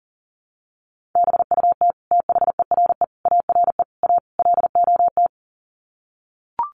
Morse code, heard as a keyed tone, about 40 wpm, 700 Hz. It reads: BUT THERE ARE A LOT
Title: Sets of 5 Words Encore2 40wpm